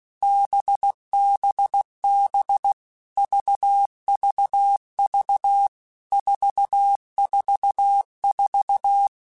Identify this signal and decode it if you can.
Morse code, heard as a keyed tone, about 16 wpm, 800 Hz. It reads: BBB VVV 444
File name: Basic Letter Groups By7Groups / VB46 / VB46@15